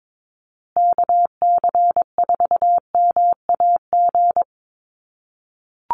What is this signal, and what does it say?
Morse code, heard as a keyed tone, about 22 wpm, 700 Hz. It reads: KC4MAG
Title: USA Call Signs Repeated 3x 22wpm